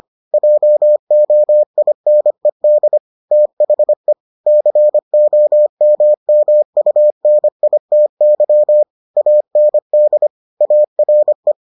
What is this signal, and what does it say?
Morse code, heard as a keyed tone, about 25 wpm, 600 Hz. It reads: JOINED THE COMMUNITY AND ARE